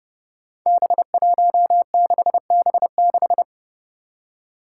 Morse code, heard as a keyed tone, about 30 wpm, 700 Hz. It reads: B16B6
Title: Five Challenging Letter-Number Combinations 30wpm